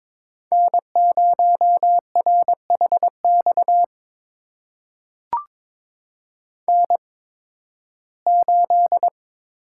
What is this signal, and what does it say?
Morse code, heard as a keyed tone, about 22 wpm, spaced as for 14 wpm, 700 Hz. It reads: N0RHX N8
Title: USA Call Signs - 5 Characters - 8x Character Spacing - 22wpm